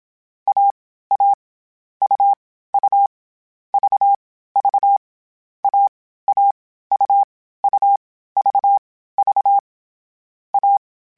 Morse code, heard as a keyed tone, about 27 wpm, 800 Hz. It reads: A A U U V V A A U U V V A
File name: Short Runs / Short Runs-mp3s / AUV AUV4 @25